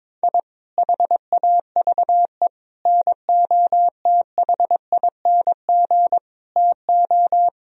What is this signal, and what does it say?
Morse code, heard as a keyed tone, about 21 wpm, 700 Hz. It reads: I HAVE NOTHING TO
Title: Sentences from Top 1000 Words - Review Part 4 - 22wpm